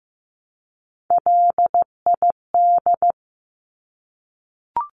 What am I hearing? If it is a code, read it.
Morse code, heard as a keyed tone, about 15 wpm, 700 Hz. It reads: LID
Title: Abbreviations Repeated 3x - 15wpm